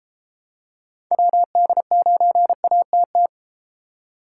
Morse code, heard as a keyed tone, about 34 wpm, 700 Hz. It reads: WD9ATT